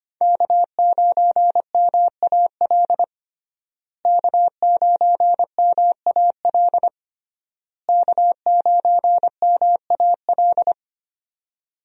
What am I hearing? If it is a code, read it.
Morse code, heard as a keyed tone, about 25 wpm, 700 Hz. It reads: K9MAL K9MAL K9MAL
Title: USA Call Signs - 5 Characters - Repeated 3x - 25wpm